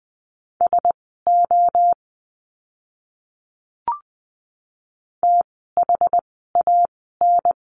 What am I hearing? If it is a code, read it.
Morse code, heard as a keyed tone, about 20 wpm, spaced as for 15 wpm, 700 Hz. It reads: SO THAN
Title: 230 Lesson 37f - Letter x - Words 20wpm